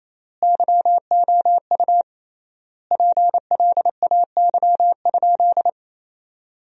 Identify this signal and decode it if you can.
Morse code, heard as a keyed tone, about 28 wpm, 700 Hz. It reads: YOU PLAY?
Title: 3x Word-Spacing - Sentences from Top 300 Words - 28wpm